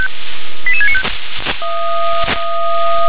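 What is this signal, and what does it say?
Five-tone selective call (ZVEI), heard as quick runs of five tones, tones of about 70 ms. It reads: 88695
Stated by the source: ZVEI five-tone call received by SDR.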